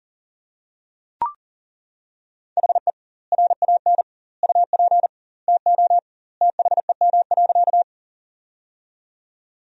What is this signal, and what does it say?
Morse code, heard as a keyed tone, about 37 wpm, 700 Hz. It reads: HE RAN UP TO THEM.
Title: 2x Word-Spacing - Sentences from Top 500 Words - 40wpm